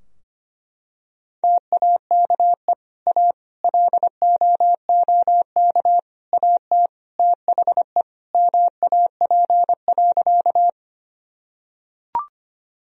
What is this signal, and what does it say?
Morse code, heard as a keyed tone, about 25 wpm, 700 Hz. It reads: TAKE A LOOK AT THE MAP.
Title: Sentences from Top 1000 Words - Review Part 2 - 25wpm